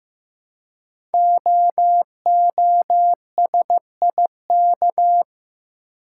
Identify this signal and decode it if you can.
Morse code, heard as a keyed tone, about 15 wpm, 700 Hz. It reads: OOSIK
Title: Code Groups - 5 Letters - 15wpm